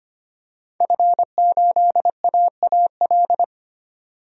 Morse code, heard as a keyed tone, about 25 wpm, 700 Hz. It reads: F8AAL
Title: DXCC-100 Call Signs - 5 Characters - 25wpm